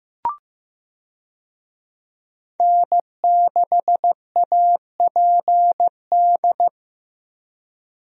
Morse code, heard as a keyed tone, about 15 wpm, 700 Hz. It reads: N6APD
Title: USA Call Signs - 5 Characters - Repeated 2x - 15wpm